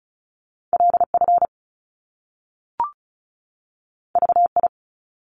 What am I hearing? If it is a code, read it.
Morse code, heard as a keyed tone, about 36 wpm, 700 Hz. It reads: LF VI
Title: Top Two Letter Combinations in Words 35wpm